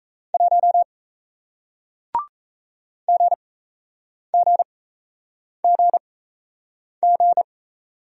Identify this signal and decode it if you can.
Morse code, heard as a keyed tone, about 38 wpm, spaced as for 8 wpm, 700 Hz. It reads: 1 GGGG